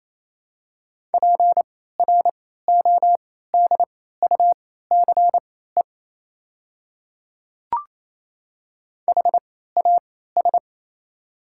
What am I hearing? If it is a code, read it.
Morse code, heard as a keyed tone, about 28 wpm, spaced as for 16 wpm, 690 Hz. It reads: PRODUCE HAS
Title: 3x Character Spacing - Top 500 Words - 28wpm